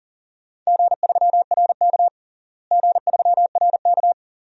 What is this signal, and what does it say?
Morse code, heard as a keyed tone, about 40 wpm, 700 Hz. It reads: G3RK G3RK
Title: DXCC-100 Call Signs - 4 Characters - Repeated 2x - 40wpm